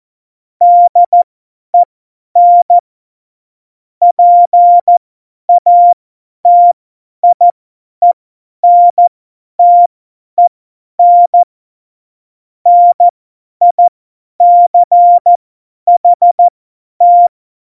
Morse code, heard as a keyed tone, about 14 wpm, spaced as for 10 wpm, 700 Hz. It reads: DEN PATIENTEN NICHT
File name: Morsekurs / Witze / Witz 15. September 2020, 50 F70